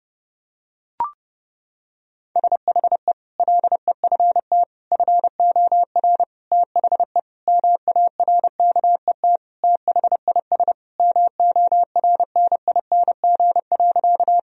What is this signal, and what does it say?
Morse code, heard as a keyed tone, about 32 wpm, 700 Hz. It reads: SHE LEFT FOR THE MARKET THIS MORNING.
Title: Sentences from Top 1000 Words - Review Part 3 - 30wpm